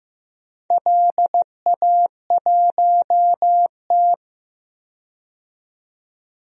Morse code, heard as a keyed tone, about 15 wpm, 700 Hz. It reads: LA1T
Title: DXCC-100 Call Signs - 4 Characters - 15wpm